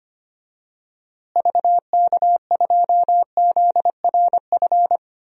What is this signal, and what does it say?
Morse code, heard as a keyed tone, about 25 wpm, 700 Hz. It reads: VK2ZRF
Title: DXCC-100 Call Signs - 6 Characters - Repeated 2x - 25wpm